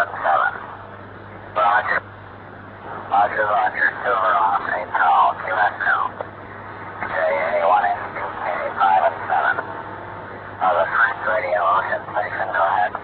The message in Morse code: none